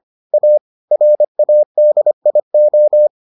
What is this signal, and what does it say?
Morse code, heard as a keyed tone, about 25 wpm, 600 Hz. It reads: A RADIO